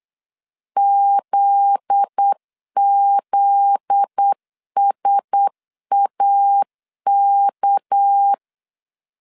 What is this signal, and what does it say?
Morse code, heard as a keyed tone, about 8 wpm, 800 Hz. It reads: ZZSAK
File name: LEZIONI / 7 GF9Z / 7 GF9Z 8wpm 3213